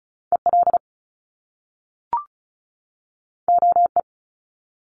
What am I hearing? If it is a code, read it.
Morse code, heard as a keyed tone, about 36 wpm, 700 Hz. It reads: EL OE
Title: Top Two Letter Combinations in Words 35wpm